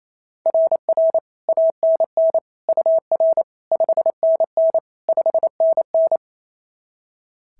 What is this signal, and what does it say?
Morse code, heard as a keyed tone, about 28 wpm, 690 Hz. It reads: RR ANN UR 5NN 5NN
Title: SOTA QSO Practice 28wpm